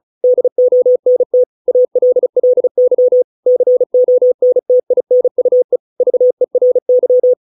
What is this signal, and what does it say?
Morse code, heard as a keyed tone, about 35 wpm, 500 Hz. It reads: DONT ALLY CONTINUE VERY